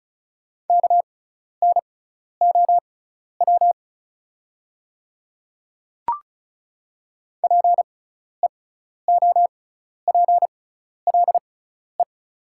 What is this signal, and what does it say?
Morse code, heard as a keyed tone, about 34 wpm, spaced as for 12 wpm, 700 Hz. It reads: KNOW PEOPLE
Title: 6x Character Spacing - Top 100 Words - 35wpm